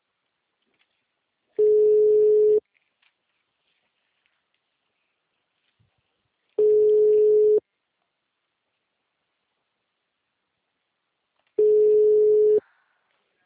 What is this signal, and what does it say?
Ringback tone (Europe, 425 Hz).
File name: out- -2021- - - .5